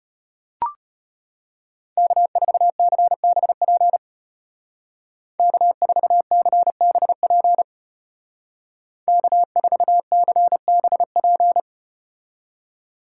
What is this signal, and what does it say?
Morse code, heard as a keyed tone, about 32 wpm, 700 Hz. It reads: K4CBP K4CBP K4CBP